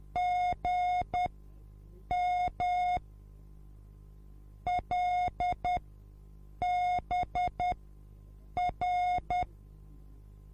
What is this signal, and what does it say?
Morse code, heard as a keyed tone, about 10 wpm, spaced as for 7 wpm, 720 Hz. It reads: GM LBR